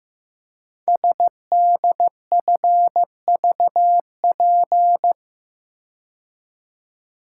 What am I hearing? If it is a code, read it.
Morse code, heard as a keyed tone, about 15 wpm, 700 Hz. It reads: SDFVP
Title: Code Groups - 5 Letter:Numbers - 15wpm